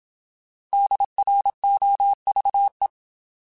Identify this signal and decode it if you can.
Morse code, heard as a keyed tone, about 27 wpm, 800 Hz. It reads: DROVE
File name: Alphabetical / Alphabetical mp3s / Ds @25